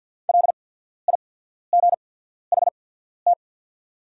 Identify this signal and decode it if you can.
Morse code, heard as a keyed tone, about 49 wpm, spaced as for 14 wpm, 690 Hz. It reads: LIGHT
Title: 8x Character Spacing - Top 300 Words - 50wpm